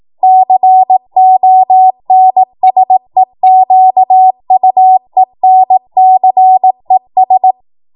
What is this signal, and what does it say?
Morse code, heard as a keyed tone, about 18 wpm, 750 Hz. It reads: CONSEQUENCES
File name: Morse / 18WPM